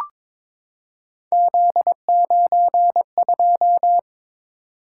Morse code, heard as a keyed tone, about 22 wpm, 700 Hz. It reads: Z92
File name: practice / 22wpm / Three Letter-Number Combinations - 22wpm